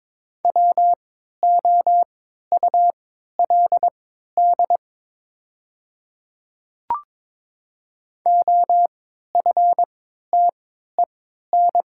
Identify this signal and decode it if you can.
Morse code, heard as a keyed tone, about 22 wpm, spaced as for 13 wpm, 690 Hz. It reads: WOULD OFTEN
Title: 3x Character Spacing - Top 300 Words - 22wpm